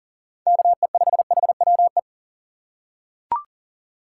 Morse code, heard as a keyed tone, about 40 wpm, 700 Hz. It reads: KE5HWE
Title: USA Call Signs - 6 Characters - 3x Character Spacing - 40wpm